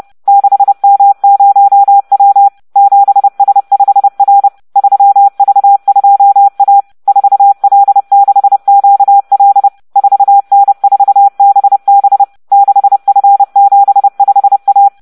Morse code, heard as a keyed tone, about 30 wpm, 800 Hz. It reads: 6M0W 7S5R 3V2A 4L6QL 4N4BB 6F75A